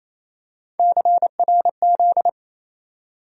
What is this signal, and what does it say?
Morse code, heard as a keyed tone, about 28 wpm, 700 Hz. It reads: CRZ
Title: USA Call Sign Postfixes - 3 Letters 28wpm